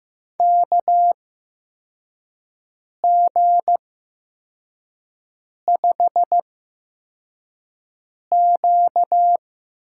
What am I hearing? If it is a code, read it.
Morse code, heard as a keyed tone, about 15 wpm, spaced as for 4 wpm, 700 Hz. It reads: KG5Q
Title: USA Call Signs - 6 Characters - 8x Character Spacing - 15wpm